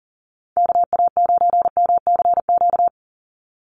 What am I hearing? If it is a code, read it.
Morse code, heard as a keyed tone, about 39 wpm, 700 Hz. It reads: KA9MCQ
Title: USA Call Signs 40wpm